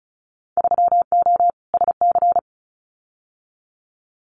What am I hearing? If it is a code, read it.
Morse code, heard as a keyed tone, about 36 wpm, 700 Hz. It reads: 3O SC
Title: Winter Field Day Exchanges Encore 35wpm